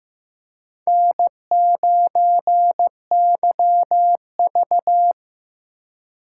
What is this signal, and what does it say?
Morse code, heard as a keyed tone, about 15 wpm, 700 Hz. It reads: N9YV
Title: USA Call Signs - 4 Characters - 4x Character Spacing - 15wpm